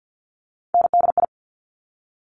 Morse code, heard as a keyed tone, about 50 wpm, 690 Hz. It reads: NDI
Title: Top Three Letter Combinations in Words 50wpm